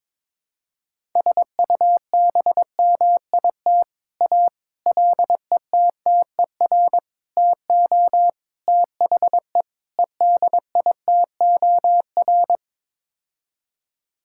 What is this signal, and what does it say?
Morse code, heard as a keyed tone, about 22 wpm, 700 Hz. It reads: SUBMIT A LETTER TO THE EDITOR
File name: practice / 22wpm / Sets of 6 Words - 22wpm